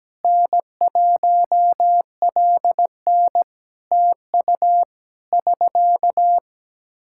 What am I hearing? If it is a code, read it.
Morse code, heard as a keyed tone, about 17 wpm, 700 Hz. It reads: N1LN TU <SK>
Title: Practice QSO - Exchange 4 - Ending - 17wpm